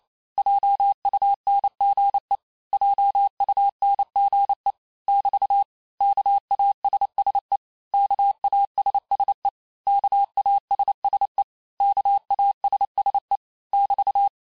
Morse code, heard as a keyed tone, about 28 wpm, 800 Hz. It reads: JUNGE JUNGE <BT> KASSE KASSE KASSE KASSE <BT>
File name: cours-du-mercredi / audios / cours-865-teste-les-mots-vitesse-27mpm